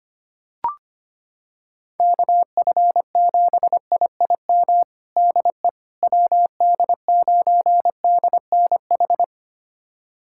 Morse code, heard as a keyed tone, about 25 wpm, 700 Hz. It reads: KF7IIM DE WD9DNH
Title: USA Callsign DE Callsign - 6 Characters - 25wpm